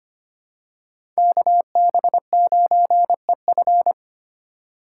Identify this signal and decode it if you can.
Morse code, heard as a keyed tone, about 25 wpm, 700 Hz. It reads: KB9EF
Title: USA Call Signs - 5 Characters - 2x Character Spacing - 25wpm